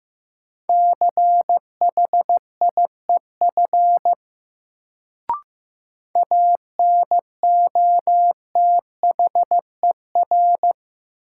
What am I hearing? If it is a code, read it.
Morse code, heard as a keyed tone, about 16 wpm, 700 Hz. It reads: CHIEF ANOTHER
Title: Sets of 5 Words Encore - New Word Drill - 15wpm